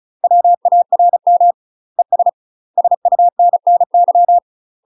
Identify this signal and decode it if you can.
Morse code, heard as a keyed tone, about 34 wpm, 700 Hz. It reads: WARM ES SUNNY
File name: UK MASTER QSOs TRAINING / 35WPM / UK MASTER QSOs 0001